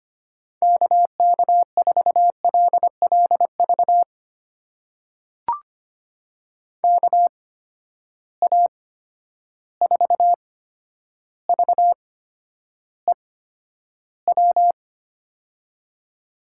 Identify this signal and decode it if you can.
Morse code, heard as a keyed tone, about 25 wpm, spaced as for 11 wpm, 700 Hz. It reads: KK4LLV KA4VEW
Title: USA Call Signs - 6 Characters - 8x Character Spacing - 25wpm